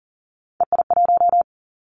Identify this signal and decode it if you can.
Morse code, heard as a keyed tone, about 37 wpm, 700 Hz. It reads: EI1
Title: Callsign Prefixes for Top 50 DXCC - 40wpm